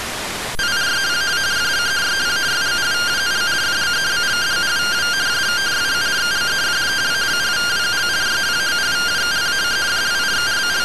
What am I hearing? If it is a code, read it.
Two-tone teleprinter signal, about 170 Hz shift.
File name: findings / SAGITTAL 43mm 4mm.gen-20 rtty rx